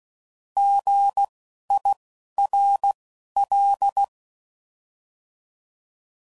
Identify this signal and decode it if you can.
Morse code, heard as a keyed tone, about 16 wpm, spaced as for 11 wpm, 800 Hz. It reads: GIRL
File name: Prefixes / By First Letter